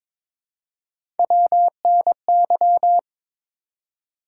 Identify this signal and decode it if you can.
Morse code, heard as a keyed tone, about 22 wpm, 700 Hz. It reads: WNY
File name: ARRL Sections v2 - 22wpm